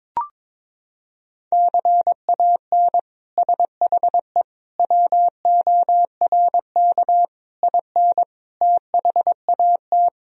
Morse code, heard as a keyed tone, about 22 wpm, 700 Hz. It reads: CAN SHE WORK IN THAT
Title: Sentences from Top 1000 Words - Review Part 4 - 22wpm